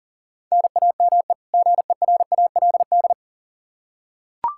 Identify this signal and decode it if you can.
Morse code, heard as a keyed tone, about 40 wpm, 700 Hz. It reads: NAME GERALD
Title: Practice QSO - Exchange 1 - Beginning - 40wpm